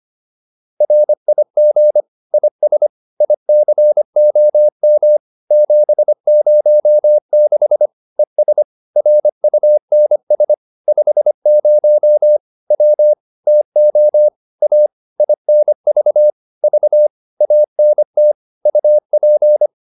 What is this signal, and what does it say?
Morse code, heard as a keyed tone, about 25 wpm, 600 Hz. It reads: RIG IS ICOM 706 ES RUNS 50 W TO A INV V ANT UP